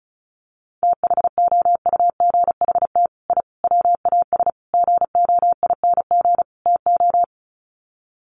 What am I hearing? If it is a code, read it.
Morse code, heard as a keyed tone, about 34 wpm, 700 Hz. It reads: THOUGHT I WAS GOING TO
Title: Sets of 5 Words Encore2 35wpm